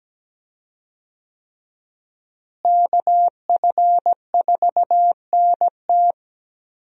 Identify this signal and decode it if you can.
Morse code, heard as a keyed tone, about 17 wpm, 700 Hz. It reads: KF4NT
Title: USA Call Signs - 5 Characters - Repeated 3x - 17wpm